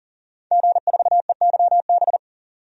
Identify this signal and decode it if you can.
Morse code, heard as a keyed tone, about 40 wpm, 700 Hz. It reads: G4EYB
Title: DXCC-100 Call Signs - 5 Characters - 40wpm